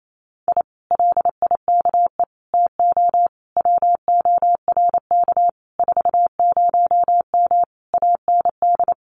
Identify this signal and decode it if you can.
Morse code, heard as a keyed tone, about 28 wpm, 700 Hz. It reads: I LIKE TO WORK 40M AND
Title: Common QSO Phrases Encore 28wpm